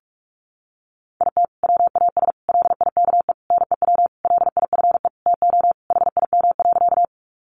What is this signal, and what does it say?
Morse code, heard as a keyed tone, about 44 wpm, 700 Hz. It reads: IT WAS LIKE NEW LIFE TO HIM.
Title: Sentences from Top 300 Words 45wpm